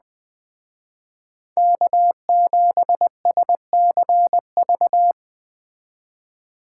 Morse code, heard as a keyed tone, about 20 wpm, 700 Hz. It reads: K7SCV
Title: Speed Racing - USA 1x3 Call Signs 17wpm